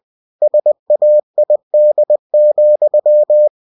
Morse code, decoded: SAID,

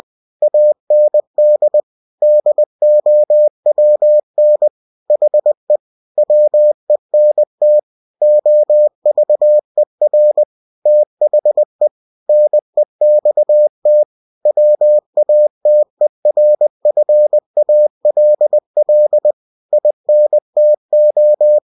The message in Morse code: AND DOWN HE WENT OVER THE NEXT WATERFALL INTO

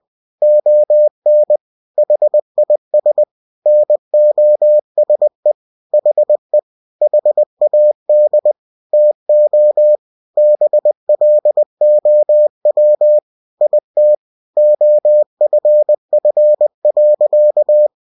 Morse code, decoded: ON HIS NOSE HE HAD TO BLOW IT OFF.

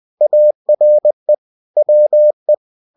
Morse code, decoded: ARE WE

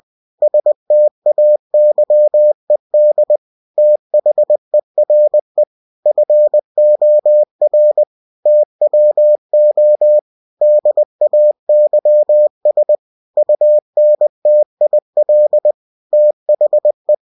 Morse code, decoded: STAYED THERE FOR TWO DAYS UNTIL THE

